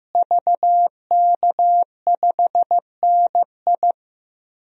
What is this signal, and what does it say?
Morse code, heard as a keyed tone, about 15 wpm, 700 Hz. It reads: VK5NI
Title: DXCC-100 Call Signs - 5 Characters - Repeated 2x - 15wpm